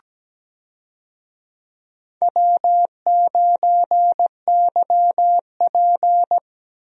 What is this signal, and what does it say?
Morse code, heard as a keyed tone, about 17 wpm, 700 Hz. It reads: W9YP
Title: USA Call Signs - 4 Characters - Repeated 3x - 17wpm